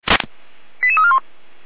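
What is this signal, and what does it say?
Five-tone selective call (ZVEI), heard as quick runs of five tones, tones of about 70 ms. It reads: 88341
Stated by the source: ZVEI five-tone selective call received on SDR, code 89080.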